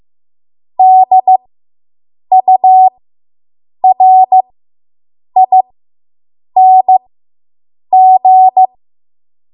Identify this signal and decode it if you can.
Morse code, heard as a keyed tone, about 15 wpm, spaced as for 7 wpm, 750 Hz. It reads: DURING